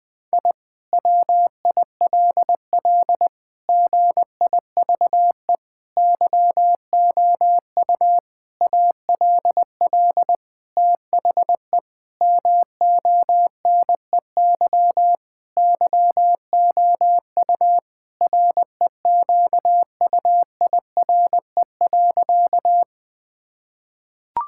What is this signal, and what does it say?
Morse code, heard as a keyed tone, about 20 wpm, 700 Hz. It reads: I WILL GIVE YOU ALL THE MONEY YOU REQUIRE.
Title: Sentences from Top 1000 Words - Review Part 6 - 20wpm